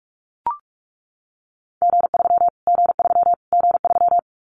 Morse code, heard as a keyed tone, about 44 wpm, 700 Hz. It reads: G3 G3 G3
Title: Callsign Prefixes for Top 25 DXCC - Repeated 3x - 45wpm